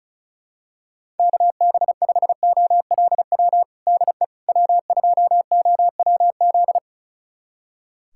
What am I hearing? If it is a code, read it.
Morse code, heard as a keyed tone, about 36 wpm, 700 Hz. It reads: KB5OLW DE W2OWZ
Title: USA Callsign DE Callsign 35wpm